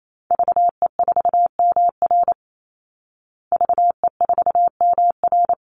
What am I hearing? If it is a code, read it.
Morse code, heard as a keyed tone, about 28 wpm, 700 Hz. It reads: VE4MR VE4MR
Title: Call Signs Repeated 3x - 28wpm